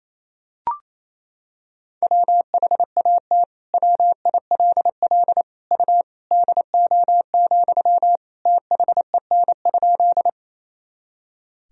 Morse code, heard as a keyed tone, about 28 wpm, 700 Hz. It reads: WHAT WILL U DO, THEN?
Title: Sentences from Top 100 Words and Abbreviations 28wpm